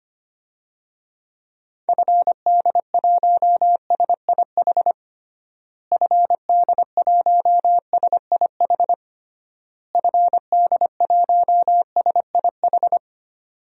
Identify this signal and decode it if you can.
Morse code, heard as a keyed tone, about 25 wpm, 700 Hz. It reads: FD1SIH FD1SIH FD1SIH
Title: DXCC-100 Call Signs - 6 Characters - Repeated 3x - 25wpm